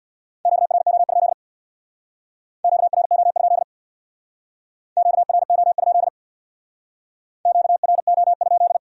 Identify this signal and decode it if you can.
Morse code, heard as a keyed tone, about 64 wpm, 700 Hz. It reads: QRO? QRO? QRO? QRO?